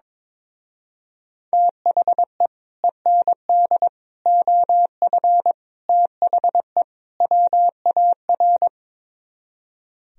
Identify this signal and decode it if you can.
Morse code, heard as a keyed tone, about 22 wpm, 700 Hz. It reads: THE END OF THE WAR
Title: Speed-Racing - Sets of 5 Words 22wpm